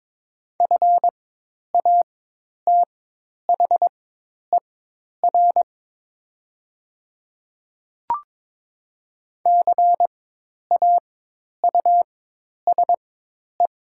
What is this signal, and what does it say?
Morse code, heard as a keyed tone, about 22 wpm, spaced as for 10 wpm, 700 Hz. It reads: FATHER CAUSE
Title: 4x Character Spacing - Top 300 Words - 22wpm